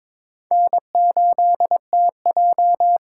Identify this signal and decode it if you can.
Morse code, heard as a keyed tone, about 26 wpm, 700 Hz. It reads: N8TJ
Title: Speed Racing - USA 1x2 Call Signs 22wpm